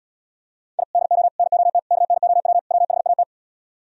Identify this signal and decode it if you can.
Morse code, heard as a keyed tone, about 77 wpm, 690 Hz. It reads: I DO NOT KNOW WHAT